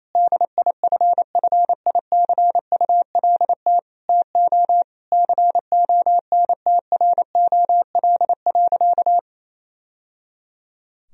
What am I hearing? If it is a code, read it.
Morse code, heard as a keyed tone, about 28 wpm, 700 Hz. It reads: DIFFICULT TO CONTROL.
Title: Sentences from Top 1000 Words - Review Part 4 - 28wpm